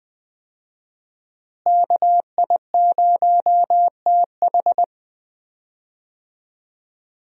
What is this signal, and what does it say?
Morse code, heard as a keyed tone, about 20 wpm, 700 Hz. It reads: KI0TH
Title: USA Call Signs - 5 Characters - Repeated 3x - 20wpm